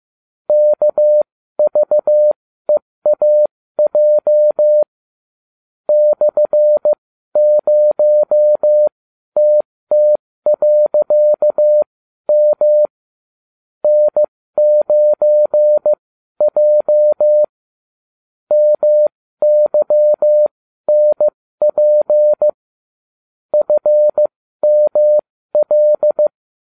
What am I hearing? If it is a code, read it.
Morse code, heard as a keyed tone, about 15 wpm, spaced as for 12 wpm, 600 Hz. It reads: KVEAJ /0TT.M N9J MYNP FML